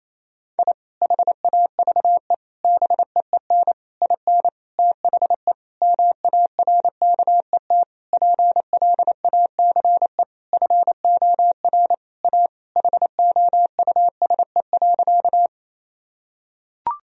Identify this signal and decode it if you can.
Morse code, heard as a keyed tone, about 27 wpm, 700 Hz. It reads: I HAVE BEEN IN THE MARKET PLACE FOR A HOUSE.
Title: Sentences from Top 1000 Words - Review Part 3 - 28wpm